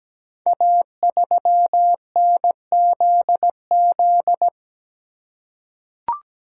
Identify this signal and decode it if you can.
Morse code, heard as a keyed tone, about 17 wpm, 690 Hz. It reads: A3NZZ
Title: Code Groups - 5 Letter:Numbers - 17wpm